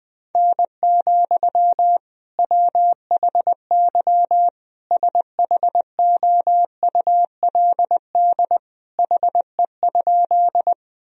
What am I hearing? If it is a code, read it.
Morse code, heard as a keyed tone, about 20 wpm, 700 Hz. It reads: N, WHY SHOULD HE?
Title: Sentences from Top 300 Words and Abbreviations 20wpm